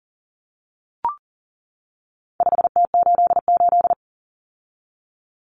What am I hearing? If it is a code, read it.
Morse code, heard as a keyed tone, about 40 wpm, 700 Hz. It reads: 5T88
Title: Common Cut Numbers - 4-Digit 40wpm